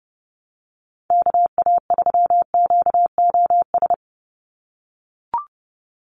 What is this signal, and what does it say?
Morse code, heard as a keyed tone, about 30 wpm, 700 Hz. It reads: KA3QOS
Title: USA Call Signs 30wpm